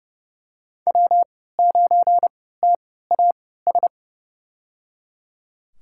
Morse code, heard as a keyed tone, about 30 wpm, spaced as for 17 wpm, 700 Hz. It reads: W9TAS
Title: USA Call Signs - 5 Characters - 3x Character Spacing - 30wpm